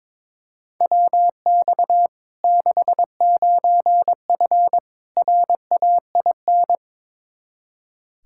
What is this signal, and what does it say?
Morse code, heard as a keyed tone, about 22 wpm, 700 Hz. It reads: WX 69F RAIN
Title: Practice QSO - Exchange 2 - Info - 22wpm